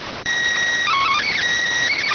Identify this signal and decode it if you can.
SSTV (slow-scan television), PD 290.